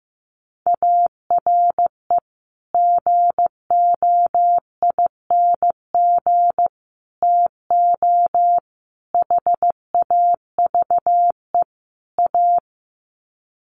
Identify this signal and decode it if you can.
Morse code, heard as a keyed tone, about 15 wpm, 700 Hz. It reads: ARE GOING TO HAVE A
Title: Sets of 5 Words Encore2 15wpm